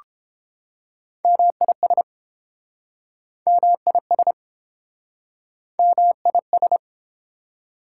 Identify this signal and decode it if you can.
Morse code, heard as a keyed tone, about 28 wpm, 690 Hz. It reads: MIS MIS MIS